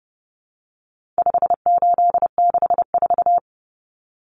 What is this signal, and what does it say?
Morse code, heard as a keyed tone, about 30 wpm, 700 Hz. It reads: 5864
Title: Numbers - 4-Digit 30wpm